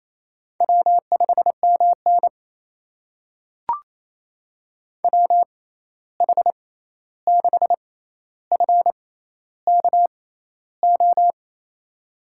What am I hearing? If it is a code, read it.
Morse code, heard as a keyed tone, about 28 wpm, spaced as for 13 wpm, 700 Hz. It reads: W5MN WH6FKO
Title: USA Call Signs - 6x Character Spacing - 28wpm